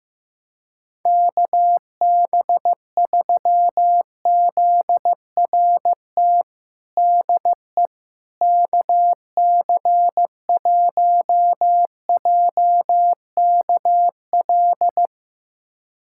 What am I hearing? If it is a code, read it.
Morse code, heard as a keyed tone, about 15 wpm, 700 Hz. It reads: KB3ZRT DE KC1JKL
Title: USA Callsign DE Callsign 15wpm